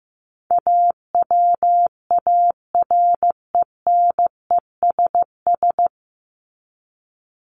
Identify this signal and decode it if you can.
Morse code, heard as a keyed tone, about 15 wpm, 700 Hz. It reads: AWARENESS
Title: Top 1501-1750 Words 15wpm